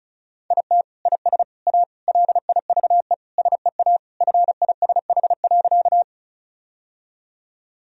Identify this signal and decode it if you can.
Morse code, heard as a keyed tone, about 36 wpm, 700 Hz. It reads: IT IS A LIVE SEA FISH.
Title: Sentences from Top 1000 Words - Review Part 1 - 35wpm